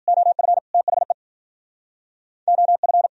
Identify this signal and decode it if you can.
Morse code, heard as a keyed tone, about 59 wpm, 700 Hz. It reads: OF THE OF